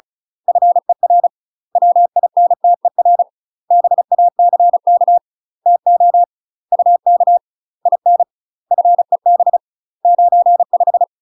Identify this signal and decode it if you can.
Morse code, heard as a keyed tone, about 36 wpm, 700 Hz. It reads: FER WINTER BACK TO UK IN FEB 95
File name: UK MASTER QSOs TRAINING / 35WPM / UK MASTER QSOs 0017